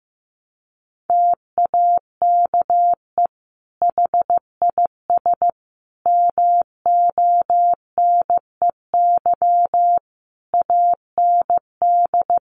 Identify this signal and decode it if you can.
Morse code, heard as a keyed tone, about 14 wpm, 700 Hz. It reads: TAKE HIS MONEY AND
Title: Sentences from Top 1300 Words 15wpm